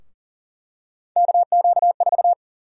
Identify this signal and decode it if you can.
Morse code, heard as a keyed tone, about 40 wpm, 700 Hz. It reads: KQ4
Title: USA Call Sign Prefixes - Repeated 2x - 40wpm